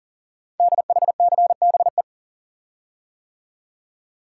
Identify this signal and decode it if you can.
Morse code, heard as a keyed tone, about 42 wpm, 690 Hz. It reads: DHCBE
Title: Code Groups - 5 Letters - 40wpm